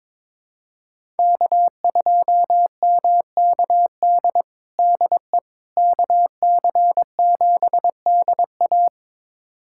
Morse code, heard as a keyed tone, about 22 wpm, 700 Hz. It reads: K2MKD DE KC7DA
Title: USA Callsign DE Callsign - 5 Characters - 22wpm